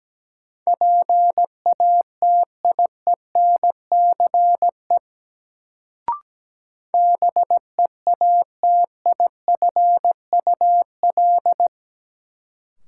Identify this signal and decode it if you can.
Morse code, heard as a keyed tone, about 17 wpm, 700 Hz. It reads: PATIENCE BEATIFUL
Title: Sentences from Top 1300 Words - New Word Drill - 17wpm